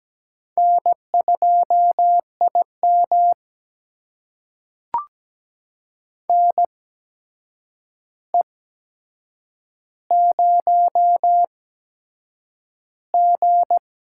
Morse code, heard as a keyed tone, about 17 wpm, spaced as for 7 wpm, 700 Hz. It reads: N2IM NE0G
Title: USA Call Signs - 4 Characters - 8x Character Spacing - 17wpm